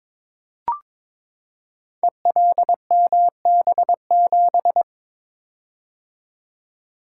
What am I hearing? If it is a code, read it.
Morse code, heard as a keyed tone, about 22 wpm, 700 Hz. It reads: ELMB7
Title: Code Groups - 5 Letter:Numbers - 22wpm